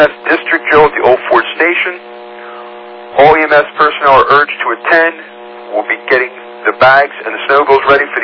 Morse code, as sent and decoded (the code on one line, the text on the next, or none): none